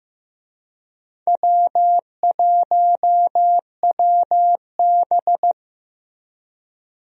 Morse code, heard as a keyed tone, about 15 wpm, 700 Hz. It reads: W1WB
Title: USA Call Signs - 4 Characters - Repeated 3x - 15wpm